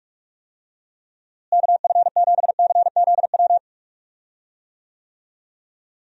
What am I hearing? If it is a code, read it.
Morse code, heard as a keyed tone, about 46 wpm, 700 Hz. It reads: KF7CZW
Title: USA Call Signs - 6 Characters - Repeated 3x - 45wpm